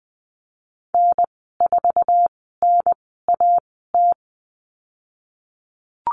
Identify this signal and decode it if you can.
Morse code, heard as a keyed tone, about 20 wpm, spaced as for 15 wpm, 700 Hz. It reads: N4NAT
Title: 062 Lesson 11g - Letter h - Call Signs 20wpm